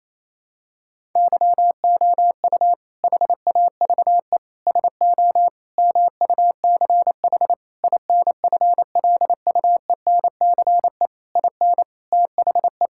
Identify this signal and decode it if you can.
Morse code, heard as a keyed tone, about 28 wpm, 700 Hz. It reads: YOU HAVE SO MUCH INFLUENCE IN THE
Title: Sentences from Top 1600 Words 28wpm